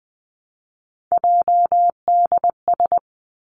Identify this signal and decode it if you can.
Morse code, heard as a keyed tone, about 20 wpm, 700 Hz. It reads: JDS
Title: Three Challenging Letter-Number Combinations 20wpm